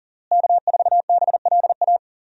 Morse code, heard as a keyed tone, about 40 wpm, 700 Hz. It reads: K4BLA
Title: USA Call Signs - 5 Characters - Repeated 3x - 40wpm